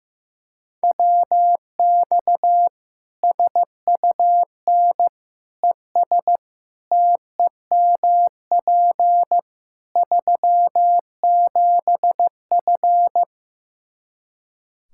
Morse code, heard as a keyed tone, about 15 wpm, 700 Hz. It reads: WX SUN ES TEMP 37F
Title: Practice QSO - Exchange 2 - Info - 15wpm